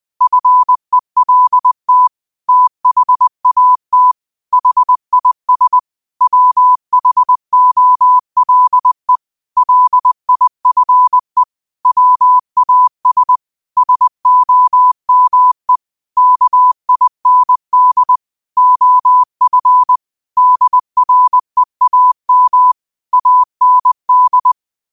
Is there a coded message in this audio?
Yes: Morse code from 0.2 to 24.5 s, heard as a keyed tone, about 20 wpm, 1000 Hz. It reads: FELT THAT HIS WHOLE LIFE WAS SOME KIND OF DREAM AND